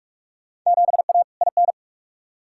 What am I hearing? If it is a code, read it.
Morse code, heard as a keyed tone, about 46 wpm, 700 Hz. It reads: 7A IN
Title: Field Day Exchanges Encore - 45wpm